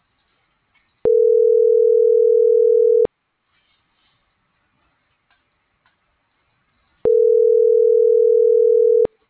Ringback tone (North America), 440 + 480 Hz.